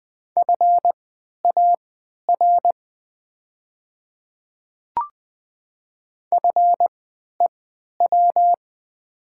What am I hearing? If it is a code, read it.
Morse code, heard as a keyed tone, about 20 wpm, spaced as for 11 wpm, 700 Hz. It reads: FAR FEW
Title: 3x Character Spacing - Top 300 Words - 20wpm